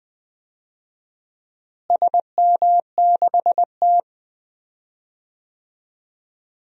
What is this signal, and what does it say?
Morse code, heard as a keyed tone, about 20 wpm, 700 Hz. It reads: SM6T
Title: DXCC-100 Call Signs - 4 Characters - Repeated 3x - 20wpm